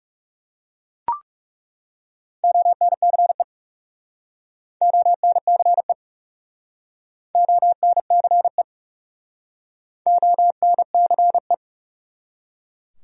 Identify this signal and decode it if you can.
Morse code, heard as a keyed tone, about 37 wpm, 700 Hz. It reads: ONCE ONCE ONCE ONCE
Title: Speed-Racing Top 300 Words 30wpm